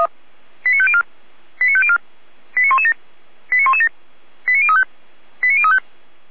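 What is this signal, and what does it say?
Five-tone selective call (ZVEI), heard as quick runs of five tones, tones of about 70 ms. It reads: 89504 89504 89117 89117 89035 89035
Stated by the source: ZVEI five-tone selective call 89115 received on an SDR.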